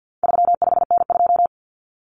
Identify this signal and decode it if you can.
Morse code, heard as a keyed tone, about 50 wpm, 700 Hz. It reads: 35N2
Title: Common Cut Numbers - 4-Digit 50wpm